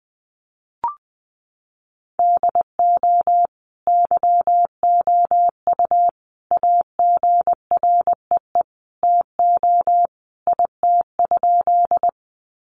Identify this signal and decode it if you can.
Morse code, heard as a keyed tone, about 20 wpm, 700 Hz. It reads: DO YOU AGREE TO IT?
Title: Sentences from Top 1100 Words 20wpm